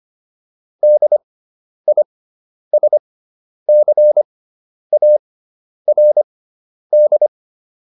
Morse code, heard as a keyed tone, about 25 wpm, spaced as for 10 wpm, 600 Hz. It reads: DISCARD